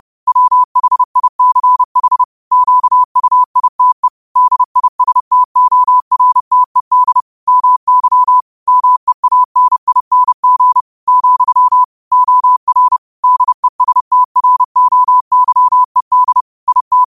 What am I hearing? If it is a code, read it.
Morse code, heard as a keyed tone, about 30 wpm, 1000 Hz. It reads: WHICH QUITE DISTORTED MY MEANING , OR DESTROYED IT
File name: qotdfiles / 30wpm